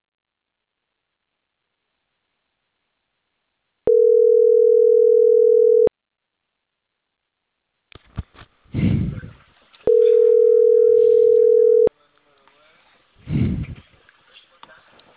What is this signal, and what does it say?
Ringback tone (North America), 440 + 480 Hz.